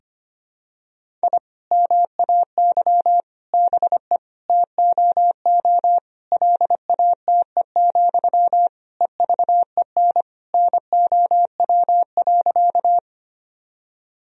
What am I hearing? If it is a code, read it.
Morse code, heard as a keyed tone, about 25 wpm, 700 Hz. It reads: I MAY BE TOO LATE, EVEN NOW.
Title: Sentences from Top 1000 Words - Review Part 1 - 25wpm